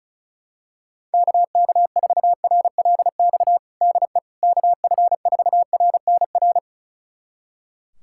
Morse code, heard as a keyed tone, about 36 wpm, 700 Hz. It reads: KK4RLX DE KF4RNR
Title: USA Callsign DE Callsign - 6 Characters - 35wpm